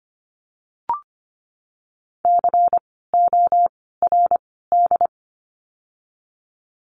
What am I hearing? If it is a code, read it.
Morse code, heard as a keyed tone, about 25 wpm, 700 Hz. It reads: C O R D
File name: course / 25:10wpm / 094 Lesson 16f - Letter c - Words 25wpm